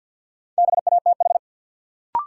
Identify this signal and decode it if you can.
Morse code, heard as a keyed tone, about 52 wpm, 700 Hz. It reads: BATH
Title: Sentences from Top 1600 Words - New Word Drill - 50wpm